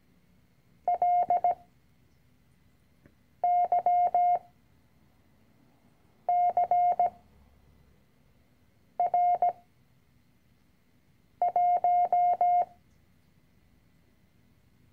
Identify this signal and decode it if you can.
Morse code, heard as a keyed tone, about 17 wpm, 710 Hz. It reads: L Y C R 1
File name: CW-Online-Kurs / Anfänger / Lektion 010